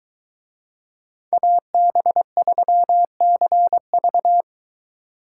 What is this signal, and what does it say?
Morse code, heard as a keyed tone, about 21 wpm, 700 Hz. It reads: AB3CV